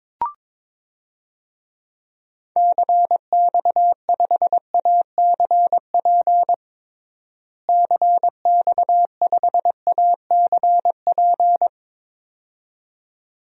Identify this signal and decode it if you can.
Morse code, heard as a keyed tone, about 22 wpm, 700 Hz. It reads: CX5ACP CX5ACP
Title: DXCC-100 Call Signs - 6 Characters - Repeated 2x - 22wpm